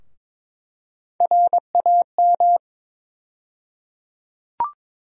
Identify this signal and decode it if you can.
Morse code, heard as a keyed tone, about 22 wpm, 700 Hz. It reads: RAM
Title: USA Call Sign Postfixes - 3 Letters 22wpm